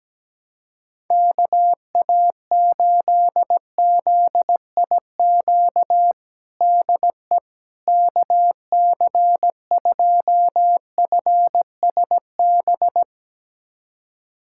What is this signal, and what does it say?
Morse code, heard as a keyed tone, about 17 wpm, 700 Hz. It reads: KA8ZIQ DE KC2FSB
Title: USA Callsign DE Callsign 17wpm